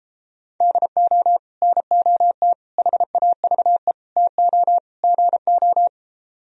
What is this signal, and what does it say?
Morse code, heard as a keyed tone, about 33 wpm, 700 Hz. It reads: DO NOT HAVE TO GO